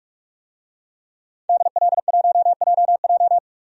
Morse code, heard as a keyed tone, about 43 wpm, 700 Hz. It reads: DL1JJ
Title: DXCC-100 Call Signs - 5 Characters - 45wpm